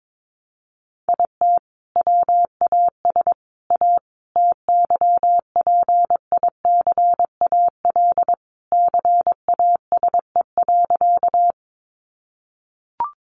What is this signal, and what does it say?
Morse code, heard as a keyed tone, about 22 wpm, 700 Hz. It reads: IT WAS A TYPICAL CASE.
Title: Sentences from Top 1100 Words 22wpm